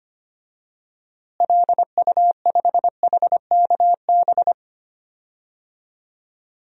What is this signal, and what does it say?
Morse code, heard as a keyed tone, about 25 wpm, 700 Hz. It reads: LU5HKB
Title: DXCC-100 Call Signs - 6 Characters - Repeated 3x - 25wpm